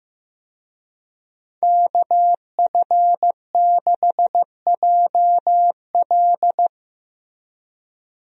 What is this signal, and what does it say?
Morse code, heard as a keyed tone, about 15 wpm, 700 Hz. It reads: KF6JL
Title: USA Call Signs - 5 Characters - Repeated 3x - 15wpm